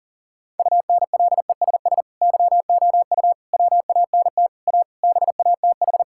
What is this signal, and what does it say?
Morse code, heard as a keyed tone, about 40 wpm, 700 Hz. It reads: UNLESS YOU WANT A BATH
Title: Sentences from Top 1600 Words 40wpm